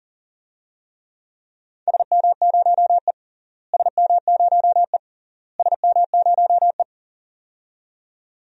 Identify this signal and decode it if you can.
Morse code, heard as a keyed tone, about 40 wpm, 700 Hz. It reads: SM0E SM0E SM0E
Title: DXCC-100 Call Signs - 4 Characters - Repeated 3x - 40wpm